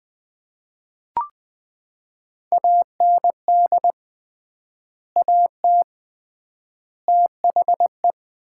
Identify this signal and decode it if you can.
Morse code, heard as a keyed tone, about 21 wpm, 690 Hz. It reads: AND AT THE
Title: 3x Word-Spacing - Sets of 5 Words - 20wpm